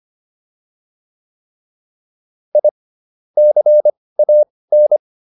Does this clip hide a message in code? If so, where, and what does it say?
Yes: Morse code from 2.5 to 5.0 s, heard as a keyed tone, about 25 wpm, spaced as for 18 wpm, 600 Hz. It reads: I CAN